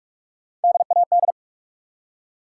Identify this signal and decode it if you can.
Morse code, heard as a keyed tone, about 44 wpm, 700 Hz. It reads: DAD
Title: Three Letter Words - 45wpm